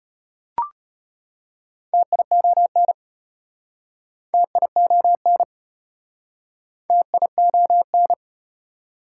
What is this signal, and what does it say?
Morse code, heard as a keyed tone, about 33 wpm, 700 Hz. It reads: TION TION TION